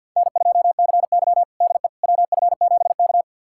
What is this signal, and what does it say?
Morse code, heard as a keyed tone, about 51 wpm, 700 Hz. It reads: N2CY DE WF7K
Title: USA Callsign DE Callsign - 4 Characters - 50wpm